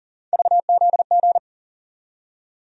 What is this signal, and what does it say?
Morse code, heard as a keyed tone, about 40 wpm, 700 Hz. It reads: VZG
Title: USA Call Sign Postfixes - 3 Letters 40wpm